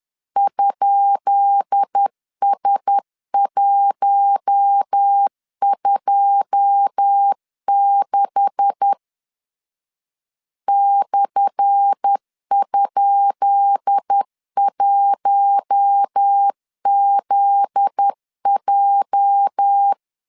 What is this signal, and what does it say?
Morse code, heard as a keyed tone, about 10 wpm, 800 Hz. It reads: ?S126 /?1ZJ
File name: LEZIONI / 10 SIGNS / 10C DOT 10wpm 2940